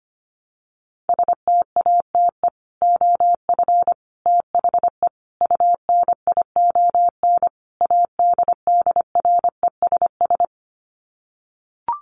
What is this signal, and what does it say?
Morse code, heard as a keyed tone, about 25 wpm, 700 Hz. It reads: STATE OF THE UNION ADDRESS
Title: Sets of 5 Words Encore 25wpm